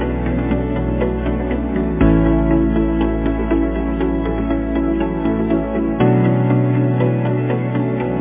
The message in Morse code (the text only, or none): none